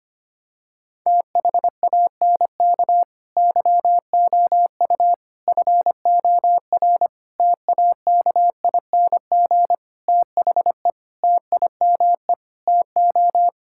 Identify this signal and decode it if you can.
Morse code, heard as a keyed tone, about 25 wpm, 700 Hz. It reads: THANK YOU FOR TAKING THE TIME TO
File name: Sets of 9 Words - 25wpm